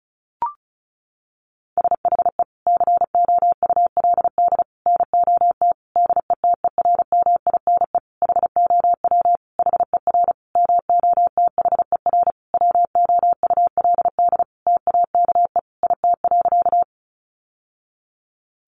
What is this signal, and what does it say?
Morse code, heard as a keyed tone, about 34 wpm, 700 Hz. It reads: SHE COULD NOT DETERMINE HOW HER MOTHER WOULD TAKE IT.
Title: Sentences from Top 700 Words 35wpm